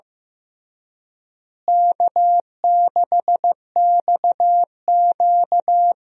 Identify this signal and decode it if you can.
Morse code, heard as a keyed tone, about 15 wpm, 700 Hz. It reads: K6XQ
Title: USA Call Signs - 4 Characters - Repeated 3x - 15wpm